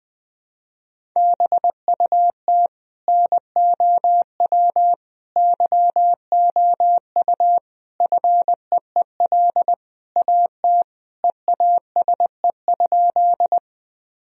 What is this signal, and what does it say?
Morse code, heard as a keyed tone, about 20 wpm, 700 Hz. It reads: BUT NOW YOU FEEL AT EASE?
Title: Sentences from Top 1000 Words - Review Part 1 - 20wpm